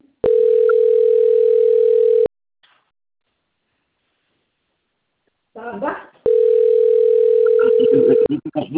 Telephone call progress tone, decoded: ringback tone

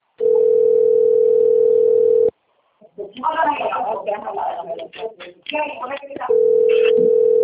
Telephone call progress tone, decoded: ringback tone

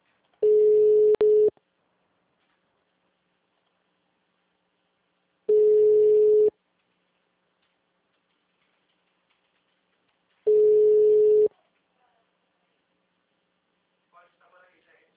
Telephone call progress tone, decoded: ringback tone

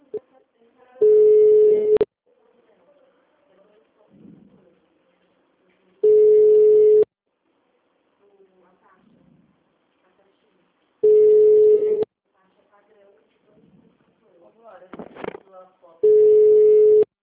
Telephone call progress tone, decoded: ringback tone

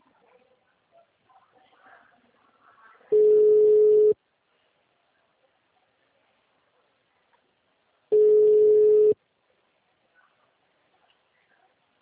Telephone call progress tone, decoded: ringback tone

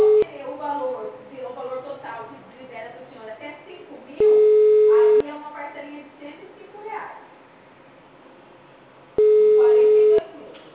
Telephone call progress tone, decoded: ringback tone

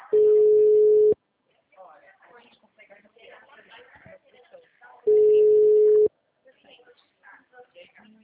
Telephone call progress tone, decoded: ringback tone

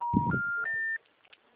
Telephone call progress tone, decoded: special information tone

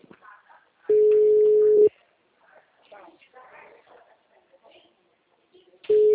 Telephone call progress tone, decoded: ringback tone

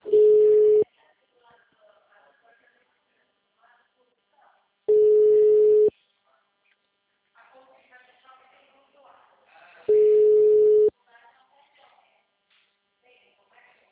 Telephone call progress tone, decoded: ringback tone

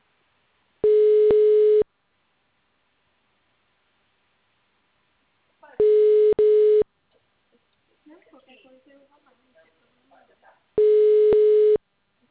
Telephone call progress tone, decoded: ringback tone

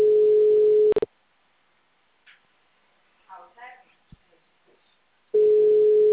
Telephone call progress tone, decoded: ringback tone

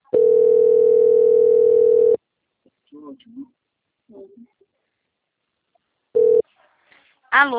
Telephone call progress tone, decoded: ringback tone

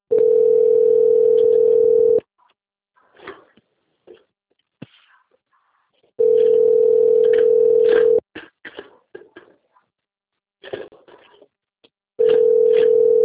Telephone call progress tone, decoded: ringback tone

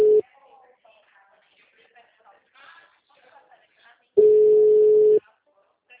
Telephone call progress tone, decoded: ringback tone

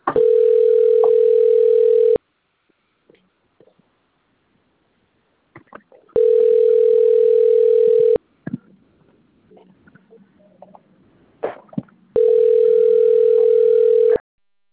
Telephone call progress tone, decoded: ringback tone

